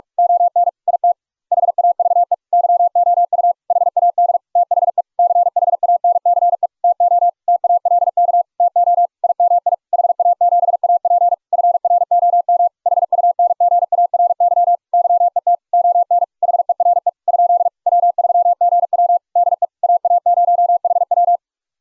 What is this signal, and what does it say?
Morse code, heard as a keyed tone, about 44 wpm, 700 Hz. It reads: ON IT HAVE YOU HAD THE CHANCE TO TALK TO IMI HA7AP FROM HUNGARY YET ON HERE ? W3GW DE AA0HW